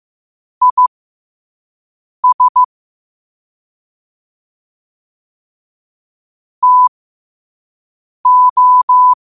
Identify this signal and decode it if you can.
Morse code, heard as a keyed tone, about 15 wpm, spaced as for 5 wpm, 1000 Hz. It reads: IS TO
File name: qotdfiles / 5wpm